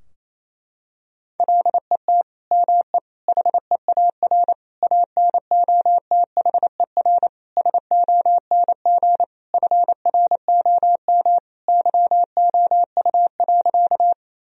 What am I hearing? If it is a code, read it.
Morse code, heard as a keyed tone, about 28 wpm, 700 Hz. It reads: LET ME HEAR ANOTHER SONG FROM YOU.
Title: Sentences from Top 1000 Words - Review Part 2 - 28wpm